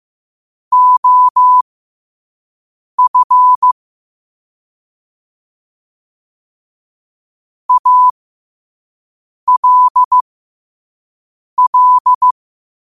Morse code, heard as a keyed tone, about 15 wpm, spaced as for 5 wpm, 1000 Hz. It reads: OF ALL
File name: qotdfiles / 5wpm